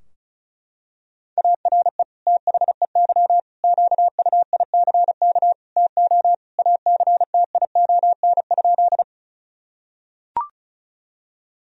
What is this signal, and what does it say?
Morse code, heard as a keyed tone, about 36 wpm, 700 Hz. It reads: ARE THEY QUICK TO ACTION?
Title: Sentences from Top 1000 Words - Review Part 4 - 35wpm